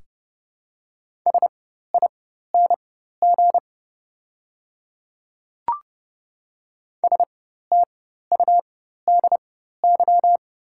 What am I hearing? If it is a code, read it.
Morse code, heard as a keyed tone, about 30 wpm, spaced as for 14 wpm, 690 Hz. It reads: SING STUDY
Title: 4x Character Spacing - Top 500 Words - 30wpm